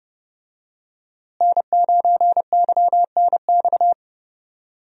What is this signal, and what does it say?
Morse code, heard as a keyed tone, about 29 wpm, 700 Hz. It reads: N9YNX
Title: USA Call Signs - 5 Characters - Repeated 3x - 30wpm